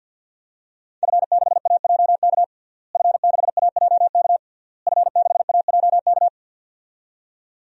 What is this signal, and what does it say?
Morse code, heard as a keyed tone, about 49 wpm, 700 Hz. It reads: F6AJK F6AJK F6AJK
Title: DXCC-100 Call Signs - 5 Characters - Repeated 3x - 50wpm